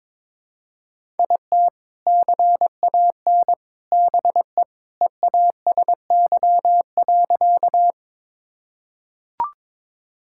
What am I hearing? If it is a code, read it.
Morse code, heard as a keyed tone, about 22 wpm, 700 Hz. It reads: IT CAN BE EASY.
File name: practice / 22wpm / Sentences with 3-Word Phrases - 22wpm